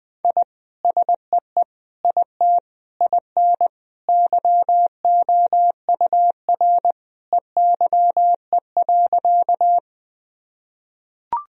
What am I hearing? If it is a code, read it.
Morse code, heard as a keyed tone, about 20 wpm, 700 Hz. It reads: I SEE IT IN YOUR EYE.
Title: Sentences from Top 1000 Words - Review Part 1 - 20wpm